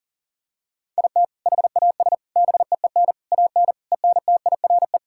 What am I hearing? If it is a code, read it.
Morse code, heard as a keyed tone, about 40 wpm, 700 Hz. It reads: IT HAS BEEN AN ENTIRE
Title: Sentences from Top 1000 Words - Review Part 5 - 40wpm